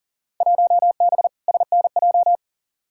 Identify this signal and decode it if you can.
Morse code, heard as a keyed tone, about 40 wpm, 700 Hz. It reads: 1B SNJ
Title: Field Day Exchanges Encore - 40wpm